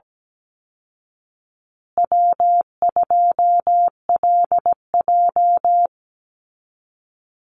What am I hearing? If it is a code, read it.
Morse code, heard as a keyed tone, about 17 wpm, 700 Hz. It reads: W2LJ
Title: Easy Call Signs Repeated 2x - 17wpm